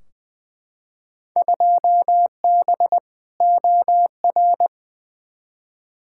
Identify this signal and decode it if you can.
Morse code, heard as a keyed tone, about 20 wpm, 700 Hz. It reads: 2B OR
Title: Field Day Exchanges Encore - 20wpm